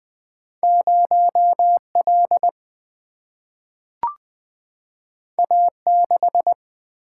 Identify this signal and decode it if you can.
Morse code, heard as a keyed tone, about 20 wpm, 700 Hz. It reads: 0 L A 6
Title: Single-Letter Single-Number 20wpm